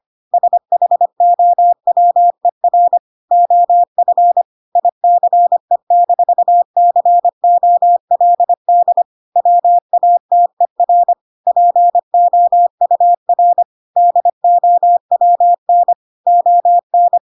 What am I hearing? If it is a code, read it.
Morse code, heard as a keyed tone, about 25 wpm, 700 Hz. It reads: SHOWER OF ICE-COLD WATER POUR DOWN ON